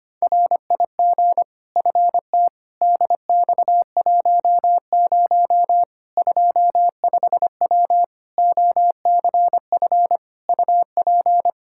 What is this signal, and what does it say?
Morse code, heard as a keyed tone, about 25 wpm, 700 Hz. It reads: RIG FT DX10 25W OCF UP
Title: Practice QSO - Exchange 2 - Info - 25wpm